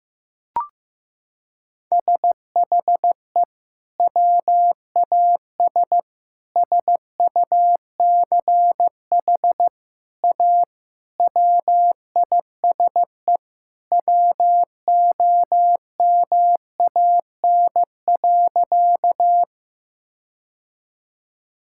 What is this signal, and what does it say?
Morse code, heard as a keyed tone, about 16 wpm, 700 Hz. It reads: SHE WAS SUCH A WISE WOMAN.
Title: Sentences from Top 1600 Words 15wpm